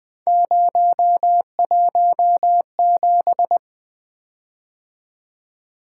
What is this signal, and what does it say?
Morse code, heard as a keyed tone, about 20 wpm, 700 Hz. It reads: 017
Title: Three Letter-Number Combinations - 20wpm